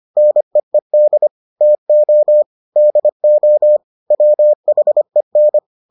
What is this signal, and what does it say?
Morse code, heard as a keyed tone, about 25 wpm, 600 Hz. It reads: NEED TO DO WHEN